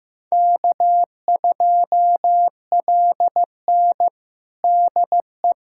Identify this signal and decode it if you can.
Morse code, heard as a keyed tone, about 16 wpm, 700 Hz. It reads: K2LN DE
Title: USA Callsign DE Callsign 15wpm